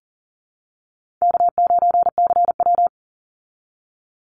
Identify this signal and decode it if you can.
Morse code, heard as a keyed tone, about 40 wpm, 700 Hz. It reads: K9CW
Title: Easy Call Signs Repeated 2x - 40wpm